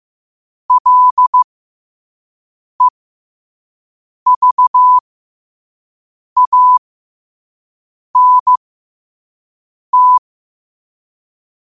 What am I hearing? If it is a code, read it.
Morse code, heard as a keyed tone, about 15 wpm, spaced as for 5 wpm, 1000 Hz. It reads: LEVANT
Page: Quotes for Tue, 12 Aug 2025 in Morse Code at 5 words per minute.